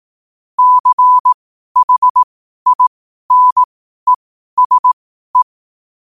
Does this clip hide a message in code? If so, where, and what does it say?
Yes: Morse code from 0.6 to 5.4 s, heard as a keyed tone, about 18 wpm, spaced as for 12 wpm, 1000 Hz. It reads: CHINESE